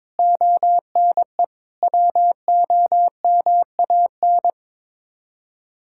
Morse code, heard as a keyed tone, about 22 wpm, 700 Hz. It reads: ONE WOMAN
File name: practice / 22wpm / Sets of 6 Words - 22wpm